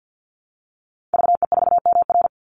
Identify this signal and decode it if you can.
Morse code, heard as a keyed tone, about 50 wpm, 700 Hz. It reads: VE4MR
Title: Call Signs - 50wpm